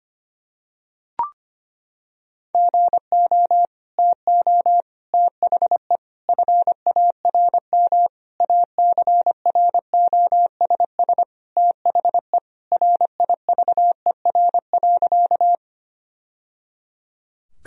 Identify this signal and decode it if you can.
Morse code, heard as a keyed tone, about 25 wpm, 700 Hz. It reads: GO TO THE FARM ACROSS THE RIVER.
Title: Sentences from Top 1000 Words - Review Part 2 - 25wpm